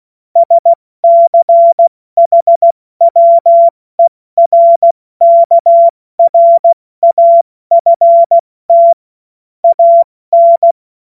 Morse code, heard as a keyed tone, about 16 wpm, spaced as for 14 wpm, 680 Hz. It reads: SCHWERKRAFT AN